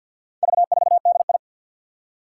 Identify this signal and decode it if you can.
Morse code, heard as a keyed tone, about 50 wpm, 700 Hz. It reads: V4DI
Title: Four Challenging Letter-Number Combinations 50wpm